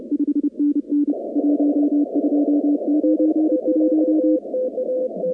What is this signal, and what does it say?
Morse code, heard as a keyed tone, about 28 wpm, 290 Hz. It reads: 5NN 1291
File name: contests / 2019-cw-wpx-kt5j-k5pi- -2330 0-21